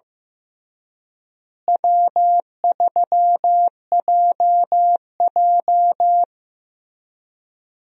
Morse code, heard as a keyed tone, about 15 wpm, 700 Hz. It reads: W3JJ
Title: USA Call Signs - 4 Characters - Repeated 3x - 15wpm